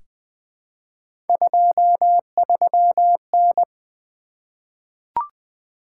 Morse code, heard as a keyed tone, about 21 wpm, 700 Hz. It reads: 23N
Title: Three Letter-Number Combinations - 20wpm